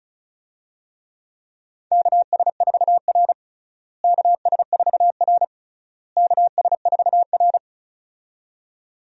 Morse code, heard as a keyed tone, about 36 wpm, 700 Hz. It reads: KS4R KS4R KS4R
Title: USA Call Signs - 4 Characters - Repeated 3x - 35wpm